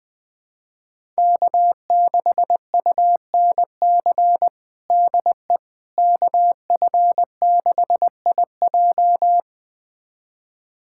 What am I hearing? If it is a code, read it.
Morse code, heard as a keyed tone, about 19 wpm, 700 Hz. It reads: K6UNC DE KF6IJ
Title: USA Callsign DE Callsign - 5 Characters - 20wpm